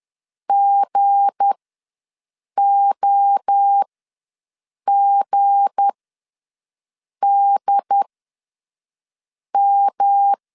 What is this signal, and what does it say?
Morse code, heard as a keyed tone, about 10 wpm, spaced as for 5 wpm, 800 Hz. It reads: GOGDM
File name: LEZIONI / 7 GF9Z / 7 G ws 10wpm 3843